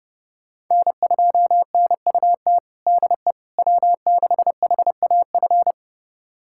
Morse code, heard as a keyed tone, about 30 wpm, 700 Hz. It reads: N2NUT DE W6HAF
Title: USA Callsign DE Callsign - 5 Characters - 30wpm